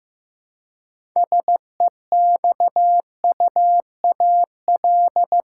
Morse code, heard as a keyed tone, about 15 wpm, 700 Hz. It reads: SEXUAL
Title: Six Letter Words - 15wpm